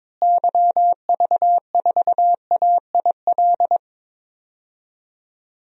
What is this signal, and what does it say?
Morse code, heard as a keyed tone, about 21 wpm, 700 Hz. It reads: YV4AIL
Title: DXCC-100 Call Signs - 6 Characters - Repeated 2x - 22wpm